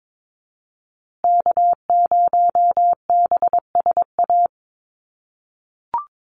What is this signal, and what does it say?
Morse code, heard as a keyed tone, about 22 wpm, 700 Hz. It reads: K0BSA
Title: USA Call Signs Repeated 3x 22wpm